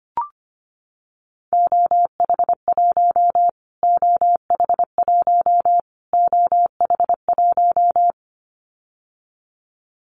Morse code, heard as a keyed tone, about 25 wpm, 700 Hz. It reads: OH1 OH1 OH1
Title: Callsign Prefixes for Top 25 DXCC - Repeated 3x - 25wpm